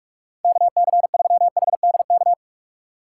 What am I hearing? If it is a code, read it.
Morse code, heard as a keyed tone, about 46 wpm, 700 Hz. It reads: KC3HDK
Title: USA Call Signs - 6 Characters - 2x Character Spacing - 45wpm